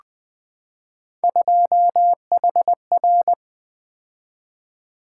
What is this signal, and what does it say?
Morse code, heard as a keyed tone, about 20 wpm, 700 Hz. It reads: 2HR
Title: Three Letter-Number Combinations - 20wpm